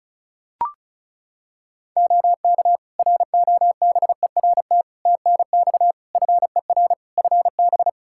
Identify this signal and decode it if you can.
Morse code, heard as a keyed tone, about 34 wpm, 700 Hz. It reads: OK ROBERT TNX FER FB
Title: Practice QSO - Exchange 4 - Ending - 35wpm